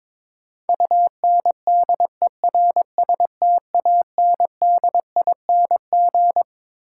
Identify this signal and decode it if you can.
Morse code, heard as a keyed tone, about 22 wpm, 700 Hz. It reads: UNDERSTANDING
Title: Sentences from Top 1000 Words - Review Part 3 - New Word Drill - 22wpm